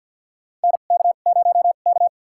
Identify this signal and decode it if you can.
Morse code, heard as a keyed tone, about 49 wpm, spaced as for 36 wpm, 690 Hz. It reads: NK0K
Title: USA Call Signs - 2x Character Spacing - 50wpm